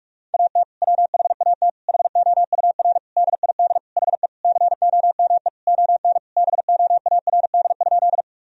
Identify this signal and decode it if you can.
Morse code, heard as a keyed tone, about 46 wpm, 700 Hz. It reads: AT WHAT HOUR DID HE COME ON BOARD?
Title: Sentences from Top 1000 Words - Review Part 4 - 45wpm